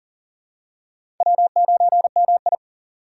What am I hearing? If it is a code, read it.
Morse code, heard as a keyed tone, about 39 wpm, 690 Hz. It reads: W9MI
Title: USA Call Signs - 4 Characters - 2x Character Spacing - 40wpm